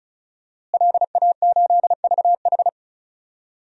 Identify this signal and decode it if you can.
Morse code, heard as a keyed tone, about 36 wpm, 700 Hz. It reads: LA8VH
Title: DXCC-100 Call Signs - 5 Characters - Repeated 2x - 35wpm